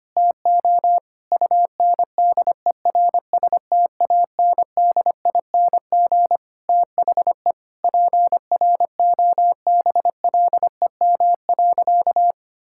Morse code, heard as a keyed tone, about 25 wpm, 700 Hz. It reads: TO UNDERSTANDING THE PROBLEM.
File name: practice / 25wpm / Sentences with 9-Word Phrases - 25wpm